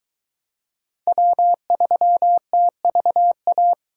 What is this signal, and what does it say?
Morse code, heard as a keyed tone, about 23 wpm, 700 Hz. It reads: W3TVA